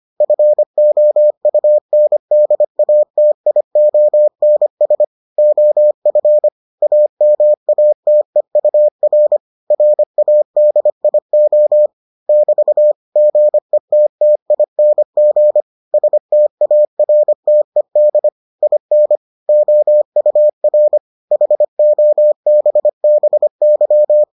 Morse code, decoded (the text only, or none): FOUNDATIONS OF AMATEUR RADIO <BT> GETTING STARTED IN OUR HOBBY